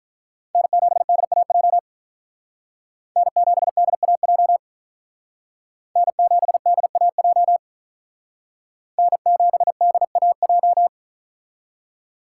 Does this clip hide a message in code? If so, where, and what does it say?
Yes: Morse code from 0.6 to 10.9 s, heard as a keyed tone, about 45 wpm, 700 Hz. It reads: N7DAJ N7DAJ N7DAJ N7DAJ